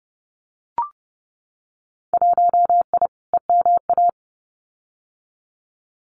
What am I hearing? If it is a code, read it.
Morse code, heard as a keyed tone, about 30 wpm, 700 Hz. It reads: 1I EMA
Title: Winter Field Day Exchanges Encore 30wpm